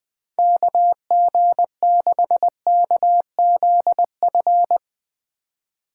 Morse code, heard as a keyed tone, about 19 wpm, 700 Hz. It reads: KG6KZF
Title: Speed Racing - USA Call Signs 17wpm